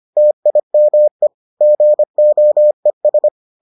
Morse code, heard as a keyed tone, about 25 wpm, 600 Hz. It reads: TIME GOES